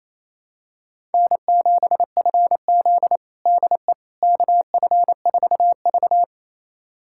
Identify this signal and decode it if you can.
Morse code, heard as a keyed tone, about 28 wpm, 700 Hz. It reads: N7FZ DE KF4V
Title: USA Callsign DE Callsign - 4 Characters - 28wpm